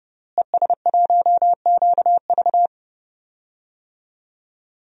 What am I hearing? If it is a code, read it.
Morse code, heard as a keyed tone, about 31 wpm, 700 Hz. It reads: ES1QV
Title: DXCC-100 Call Signs - 5 Characters - Repeated 3x - 30wpm